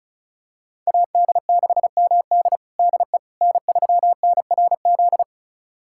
Morse code, heard as a keyed tone, about 36 wpm, 690 Hz. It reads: AD6MD DE N3NRZ
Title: USA Callsign DE Callsign - 5 Characters - 35wpm